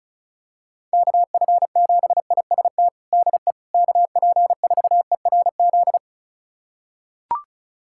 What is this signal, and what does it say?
Morse code, heard as a keyed tone, about 34 wpm, 700 Hz. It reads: KF7IST DE KP4ERZ
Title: USA Callsign DE Callsign - 6 Characters - 35wpm